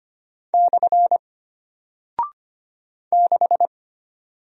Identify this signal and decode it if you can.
Morse code, heard as a keyed tone, about 25 wpm, 700 Hz. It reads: / 6
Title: Single Character (Letter:Number:Punctuation:Prosign) Alt 25wpm